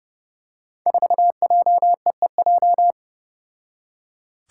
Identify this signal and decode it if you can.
Morse code, heard as a keyed tone, about 28 wpm, 700 Hz. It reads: 4JEEJ
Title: Five Challenging Letter-Number Combinations 30wpm